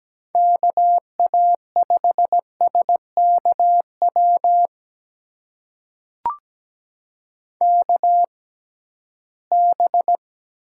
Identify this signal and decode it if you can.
Morse code, heard as a keyed tone, about 17 wpm, spaced as for 13 wpm, 700 Hz. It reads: KA5SKW KB
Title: USA Call Signs - 6 Characters - 6x Character Spacing - 17wpm